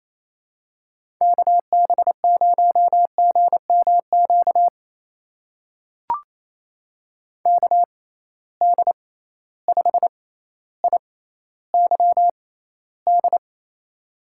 Morse code, heard as a keyed tone, about 28 wpm, spaced as for 14 wpm, 700 Hz. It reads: KB0GMQ KD5IYD
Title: USA Call Signs - 6 Characters - 6x Character Spacing - 28wpm